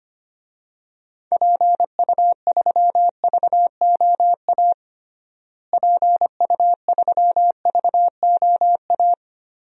Morse code, heard as a keyed tone, about 25 wpm, 700 Hz. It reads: PU3VOA PU3VOA
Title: DXCC-100 Call Signs - 6 Characters - Repeated 3x - 25wpm